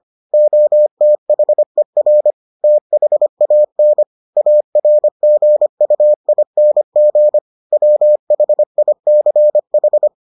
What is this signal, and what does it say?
Morse code, heard as a keyed tone, about 25 wpm, 600 Hz. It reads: OTHER THAN ARGUING WHICH